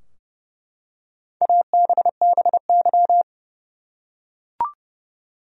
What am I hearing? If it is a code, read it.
Morse code, heard as a keyed tone, about 31 wpm, 700 Hz. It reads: ABBY
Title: USA Common Names - Max 5 Letters - 30wpm